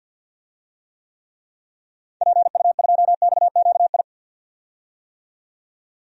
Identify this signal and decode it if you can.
Morse code, heard as a keyed tone, about 50 wpm, 700 Hz. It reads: PU2XQI
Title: DXCC-100 Call Signs - 6 Characters - 50wpm